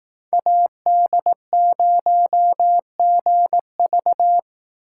Morse code, heard as a keyed tone, about 18 wpm, 700 Hz. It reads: AD0GV
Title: Speed Racing - USA 2x2 Call Signs 15wpm